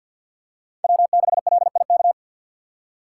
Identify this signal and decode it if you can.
Morse code, heard as a keyed tone, about 50 wpm, 700 Hz. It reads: W6LIK
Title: USA Call Signs - 4x Character Spacing - 50wpm